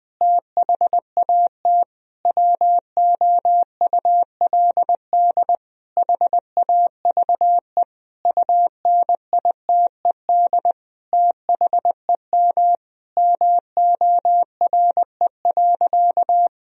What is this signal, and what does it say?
Morse code, heard as a keyed tone, about 21 wpm, 700 Hz. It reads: THAT WOULD HAVE UNITED THEM MORE.
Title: Sentences from Top 1000 Words - Review Part 4 - 20wpm